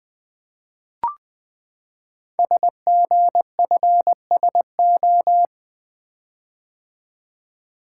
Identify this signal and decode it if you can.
Morse code, heard as a keyed tone, about 20 wpm, 700 Hz. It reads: SGFSO
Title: Code Groups - 5 Letters - 20wpm